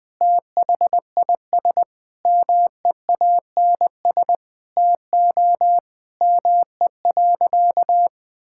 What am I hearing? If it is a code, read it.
Morse code, heard as a keyed tone, about 21 wpm, 700 Hz. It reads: THIS MEANS TO ME.
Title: Sentences with 5-Word Phrases - Spoken Beginning - 20wpm